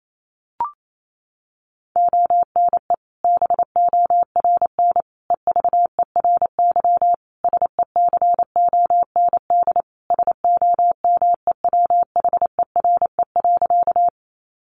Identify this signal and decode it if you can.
Morse code, heard as a keyed tone, about 28 wpm, 700 Hz. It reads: ONE BORN EVERY SECOND SOMEWHERE.
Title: Sentences from Top 1400 Words 28wpm